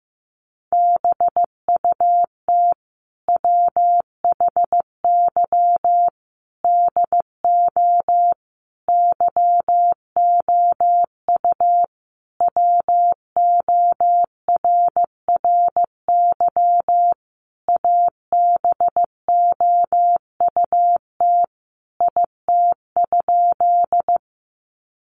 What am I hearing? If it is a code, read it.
Morse code, heard as a keyed tone, about 15 wpm, 700 Hz. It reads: BUT WHY DO YOU WORRY ABOUT IT?
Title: Sentences from Top 1100 Words 15wpm